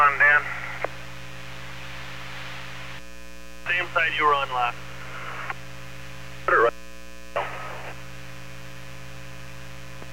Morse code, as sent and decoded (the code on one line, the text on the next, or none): none